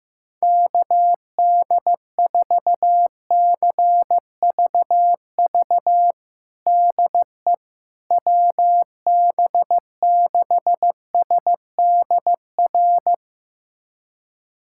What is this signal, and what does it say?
Morse code, heard as a keyed tone, about 15 wpm, 700 Hz. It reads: KD4CVV DE WB6SDR
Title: USA Callsign DE Callsign - 6 Characters - 15wpm